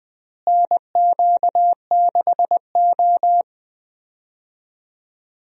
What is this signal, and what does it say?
Morse code, heard as a keyed tone, about 20 wpm, 700 Hz. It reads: NQ6O
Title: USA Call Signs - 4 Characters - Repeated 3x - 20wpm